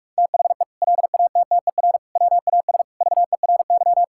Morse code, heard as a keyed tone, about 46 wpm, 700 Hz. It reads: THE LATTER WAS VERY
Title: Sentences from Top 1600 Words 45wpm